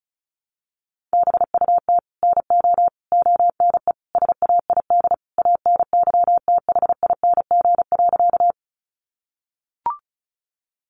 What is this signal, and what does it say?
Morse code, heard as a keyed tone, about 34 wpm, 700 Hz. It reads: BUT NO ONE SAID ANYTHING.
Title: Sentences from Top 1000 Words 35wpm